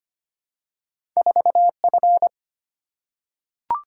Morse code, heard as a keyed tone, about 25 wpm, 690 Hz. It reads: 4F
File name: Two Letter-Number Combinations - 25wpm